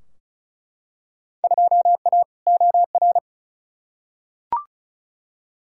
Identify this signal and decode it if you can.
Morse code, heard as a keyed tone, about 36 wpm, 700 Hz. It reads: 2A OR
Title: Field Day Exchanges - 35wpm